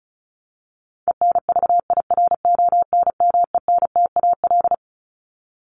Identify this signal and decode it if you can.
Morse code, heard as a keyed tone, about 34 wpm, 700 Hz. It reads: ENVIRONMENTAL
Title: Top 1001-1250 Words 35wpm